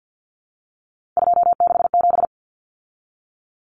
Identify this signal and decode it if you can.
Morse code, heard as a keyed tone, about 50 wpm, 700 Hz. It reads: 267
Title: Common Cut Numbers - 3-Digit 50wpm